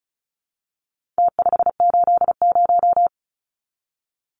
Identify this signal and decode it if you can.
Morse code, heard as a keyed tone, about 34 wpm, 700 Hz. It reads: T580
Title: Numbers - Contest Style - 4-Digit 35wpm